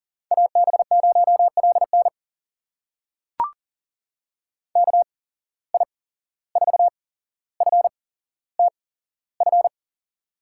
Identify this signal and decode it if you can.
Morse code, heard as a keyed tone, about 40 wpm, spaced as for 16 wpm, 700 Hz. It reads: AB0LN KI4FTF
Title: USA Call Signs - 8x Character Spacing - 40wpm